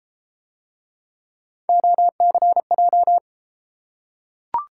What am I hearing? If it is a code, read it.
Morse code, heard as a keyed tone, about 32 wpm, 700 Hz. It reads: OCJ